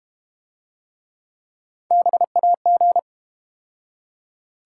Morse code, heard as a keyed tone, about 31 wpm, 700 Hz. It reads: BAG
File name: Easy Words - Speed-Racing with Hints - 25wpm